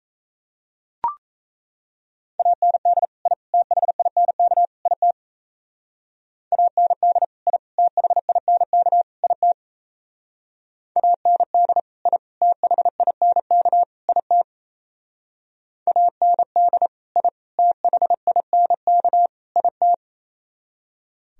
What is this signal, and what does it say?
Morse code, heard as a keyed tone, about 34 wpm, 700 Hz. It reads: AND I THINK IT AND I THINK IT AND I THINK IT AND I THINK IT
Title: Speed-Racing - Sets of 4 Words 28wpm